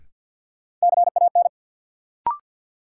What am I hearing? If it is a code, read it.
Morse code, heard as a keyed tone, about 47 wpm, 700 Hz. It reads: CAN
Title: Sentences from Top 100 Words - New Word Drill - 50wpm